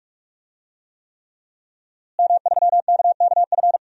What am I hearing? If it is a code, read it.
Morse code, heard as a keyed tone, about 46 wpm, 700 Hz. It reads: M3KKF
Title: DXCC-100 Call Signs - 5 Characters - Repeated 3x - 45wpm